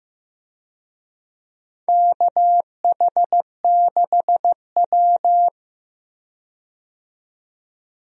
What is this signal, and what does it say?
Morse code, heard as a keyed tone, about 15 wpm, 700 Hz. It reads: KH6W
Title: DXCC-100 Call Signs - 4 Characters - 15wpm